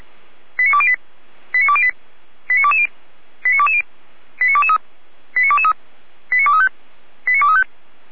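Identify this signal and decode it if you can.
Five-tone selective call (ZVEI), heard as quick runs of five tones, tones of about 70 ms. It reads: 89208 89208 89220 89220 89223 89223 89235 89235
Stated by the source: ZVEI five-tone call 89169, received by SDR.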